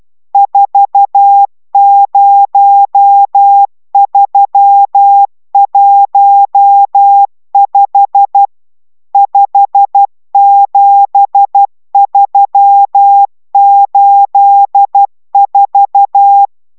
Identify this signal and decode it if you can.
Morse code, heard as a keyed tone, about 12 wpm, 800 Hz. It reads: 40315 57384